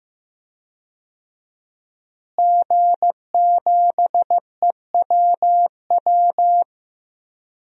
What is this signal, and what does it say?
Morse code, heard as a keyed tone, about 15 wpm, 700 Hz. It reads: G7EWW
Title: DXCC-100 Call Signs - 5 Characters - Repeated 2x - 15wpm